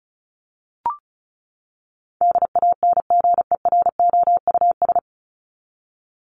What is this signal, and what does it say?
Morse code, heard as a keyed tone, about 34 wpm, 700 Hz. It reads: DANGEROUS
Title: Top 1001-1250 Words 35wpm